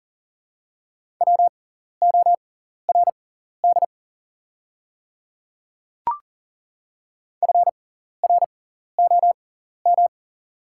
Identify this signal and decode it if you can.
Morse code, heard as a keyed tone, about 40 wpm, spaced as for 14 wpm, 700 Hz. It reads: WORD FROM
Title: 6x Character Spacing - Top 100 Words - 40wpm